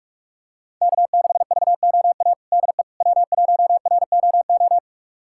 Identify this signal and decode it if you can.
Morse code, heard as a keyed tone, about 45 wpm, 700 Hz. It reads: K6VOA DE W1ROO
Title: USA Callsign DE Callsign - 5 Characters - 45wpm